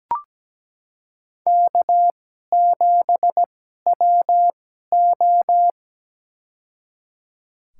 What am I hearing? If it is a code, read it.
Morse code, heard as a keyed tone, about 17 wpm, spaced as for 12 wpm, 700 Hz. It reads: K7WO
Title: USA Call Signs - 4 Characters - 2x Character Spacing - 17wpm